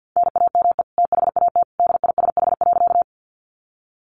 Morse code, heard as a keyed tone, about 50 wpm, 700 Hz. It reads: NAME THAT DISH.
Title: Sentences from Top 1300 Words 50wpm